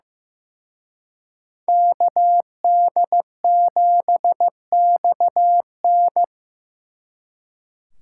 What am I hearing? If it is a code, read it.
Morse code, heard as a keyed tone, about 18 wpm, 700 Hz. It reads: KD7XN
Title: Speed Racing - USA 2x2 Call Signs 15wpm